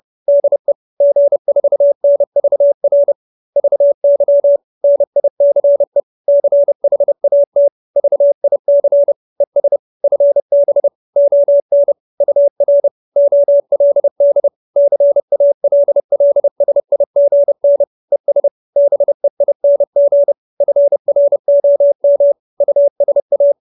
Morse code, heard as a keyed tone, about 30 wpm, 570 Hz. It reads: DE G4NVR VY NICE CHAT VIC ES FB ON UR OLD CALLSIGN ES BEING FROM USA
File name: QSO315 30